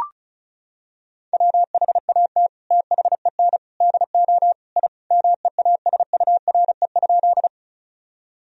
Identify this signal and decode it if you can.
Morse code, heard as a keyed tone, about 36 wpm, 700 Hz. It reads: WHAT THEN DO I MEASURE?
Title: Sentences from Top 1000 Words - Review Part 2 - 35wpm